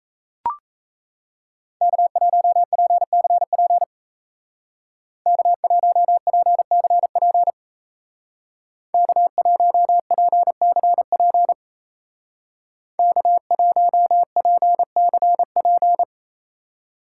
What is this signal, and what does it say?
Morse code, heard as a keyed tone, about 35 wpm, 700 Hz. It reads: K1PCP K1PCP K1PCP K1PCP
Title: Speed Racing - USA Call Signs 28wpm